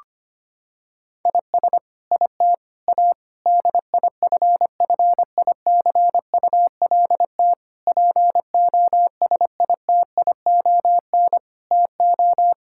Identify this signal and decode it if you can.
Morse code, heard as a keyed tone, about 25 wpm, 700 Hz. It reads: IS IT A DIFFICULT POSITION TO
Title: Sentences from Top 1000 Words - Review Part 5 - 25wpm